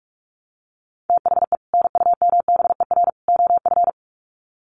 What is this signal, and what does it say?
Morse code, heard as a keyed tone, about 44 wpm, 700 Hz. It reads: THE NUMBER OF
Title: Sets of 3 Words 45wpm